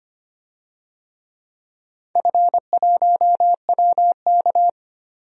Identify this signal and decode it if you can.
Morse code, heard as a keyed tone, about 25 wpm, 700 Hz. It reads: F1WK
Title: DXCC-100 Call Signs - 4 Characters - 25wpm